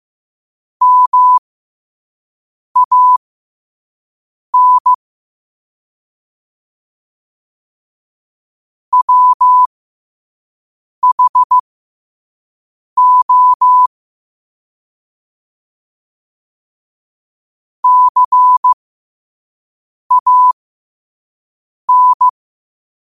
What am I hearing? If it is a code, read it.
Morse code, heard as a keyed tone, about 15 wpm, spaced as for 5 wpm, 1000 Hz. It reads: MAN WHO CAN